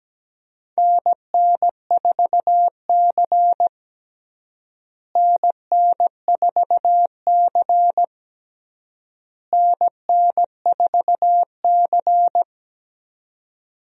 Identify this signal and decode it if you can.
Morse code, heard as a keyed tone, about 17 wpm, 700 Hz. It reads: NN4C NN4C NN4C
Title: USA Call Signs - 4 Characters - Repeated 3x - 17wpm